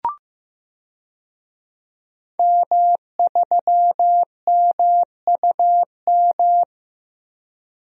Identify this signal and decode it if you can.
Morse code, heard as a keyed tone, about 15 wpm, 700 Hz. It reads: M3MUM
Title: DXCC-100 Call Signs - 5 Characters - Repeated 2x - 15wpm